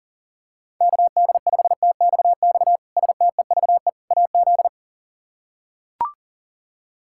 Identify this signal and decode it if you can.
Morse code, heard as a keyed tone, about 40 wpm, 700 Hz. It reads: KD5TXX STEVE AZ
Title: K1USN SST QSO Practice - 40wpm